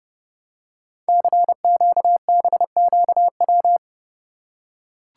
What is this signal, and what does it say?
Morse code, heard as a keyed tone, about 30 wpm, 700 Hz. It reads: CQBQW
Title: Code Groups - 5 Letter:Numbers - 30wpm